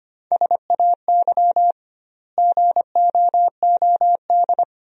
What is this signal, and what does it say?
Morse code, heard as a keyed tone, about 25 wpm, 700 Hz. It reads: SAY GOOD
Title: 2x Word-Spacing - Sentences from Top 300 Words - 25wpm